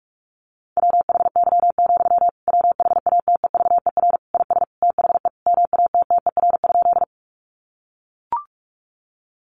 Morse code, heard as a keyed tone, about 44 wpm, 700 Hz. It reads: WHY, WHATEVER IS THE MATTER?
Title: Sentences from Top 1100 Words 45wpm